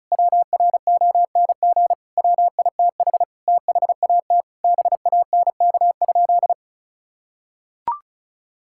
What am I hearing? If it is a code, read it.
Morse code, heard as a keyed tone, about 36 wpm, 700 Hz. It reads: WRONG WITH THAT BANK?
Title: Sentences from Top 1000 Words - Review Part 6 - 35wpm